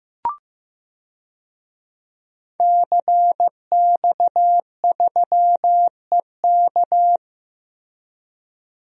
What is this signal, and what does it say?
Morse code, heard as a keyed tone, about 15 wpm, 700 Hz. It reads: CX3EK
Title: DXCC-100 Call Signs - 5 Characters - Repeated 2x - 15wpm